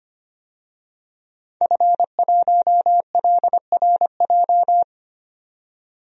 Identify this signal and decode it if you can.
Morse code, heard as a keyed tone, about 25 wpm, 700 Hz. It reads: F1LRJ
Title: DXCC-100 Call Signs - 5 Characters - 25wpm